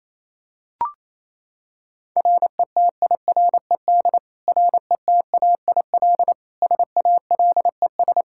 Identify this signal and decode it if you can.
Morse code, heard as a keyed tone, about 28 wpm, 700 Hz. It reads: RETIRED RETAIL SALES
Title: Practice QSO - Exchange 3 - More Info - 28wpm